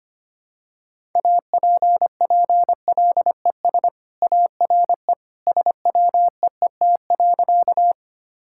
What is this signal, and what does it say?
Morse code, heard as a keyed tone, about 25 wpm, 690 Hz. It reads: APPLES ARE SWEET.
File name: Easy Sentences with Hints - 25wpm